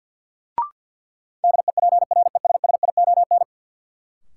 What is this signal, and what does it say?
Morse code, heard as a keyed tone, about 52 wpm, 700 Hz. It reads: DEPRESSION
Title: Sentences from Top 1300 Words - New Word Drill - 50wpm